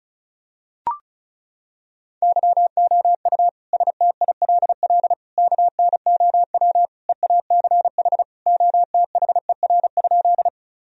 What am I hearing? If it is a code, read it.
Morse code, heard as a keyed tone, about 36 wpm, 700 Hz. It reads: YOU STILL KNOW EACH OTHER?
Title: Sentences from Top 1000 Words - Review Part 1 - 35wpm